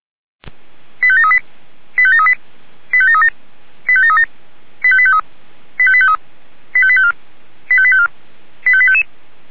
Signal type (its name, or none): five-tone selective call